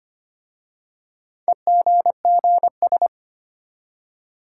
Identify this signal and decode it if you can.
Morse code, heard as a keyed tone, about 25 wpm, 700 Hz. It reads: EGGS
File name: practice / 25wpm / Easy Words with Hints - 25wpm